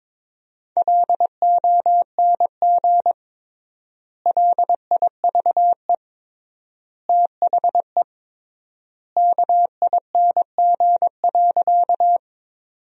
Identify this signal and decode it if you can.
Morse code, heard as a keyed tone, about 22 wpm, 700 Hz. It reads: LONG LIVE THE KING.
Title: 3x Word-Spacing - Sentences from Top 400 Words - 22wpm